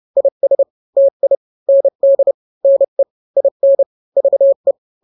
Morse code, heard as a keyed tone, about 30 wpm, 550 Hz. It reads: IS TI ND NE IN VE
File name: n-grams / 2-grams 30 25 WPM